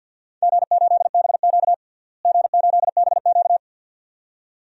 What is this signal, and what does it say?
Morse code, heard as a keyed tone, about 52 wpm, 700 Hz. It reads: G8BQ G8BQ
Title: DXCC-100 Call Signs - 4 Characters - Repeated 2x - 50wpm